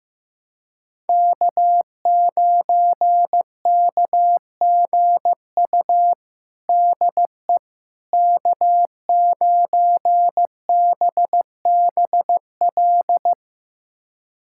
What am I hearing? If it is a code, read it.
Morse code, heard as a keyed tone, about 15 wpm, 700 Hz. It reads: K9KGU DE K9BBL
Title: USA Callsign DE Callsign - 5 Characters - 15wpm